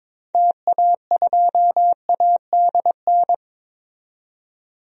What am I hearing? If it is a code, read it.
Morse code, heard as a keyed tone, about 21 wpm, 700 Hz. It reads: TA2ADN
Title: DXCC-100 Call Signs - 6 Characters - 22wpm